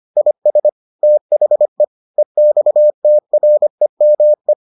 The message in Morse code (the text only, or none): IS THE EXTREME